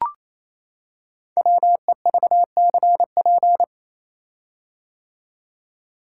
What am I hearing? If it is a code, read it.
Morse code, heard as a keyed tone, about 27 wpm, 700 Hz. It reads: WEVCP
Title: Code Groups - 5 Letter:Numbers - 28wpm